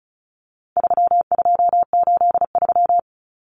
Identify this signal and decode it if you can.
Morse code, heard as a keyed tone, about 34 wpm, 700 Hz. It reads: 3283
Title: Common Cut Numbers - 4-Digit 35wpm